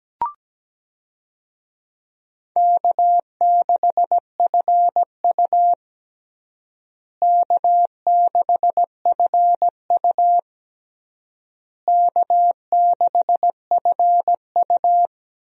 Morse code, heard as a keyed tone, about 17 wpm, 700 Hz. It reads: K6FU K6FU K6FU
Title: USA Call Signs - 4 Characters - Repeated 3x - 17wpm